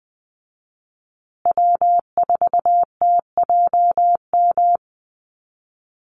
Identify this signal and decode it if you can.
Morse code, heard as a keyed tone, about 20 wpm, 700 Hz. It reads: W4TJM
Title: USA Call Signs Repeated 2x 20wpm